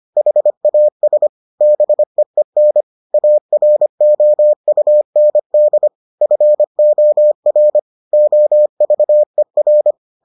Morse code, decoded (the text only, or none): HAS BEEN AROUND FOR OVER